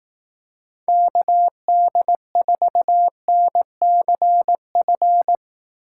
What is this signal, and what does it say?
Morse code, heard as a keyed tone, about 18 wpm, 700 Hz. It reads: KD4NCF
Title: Speed Racing - USA 2x3 Call Signs 15wpm